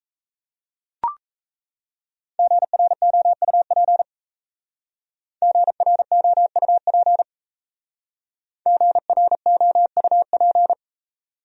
Morse code, heard as a keyed tone, about 36 wpm, 700 Hz. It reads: GROUP GROUP GROUP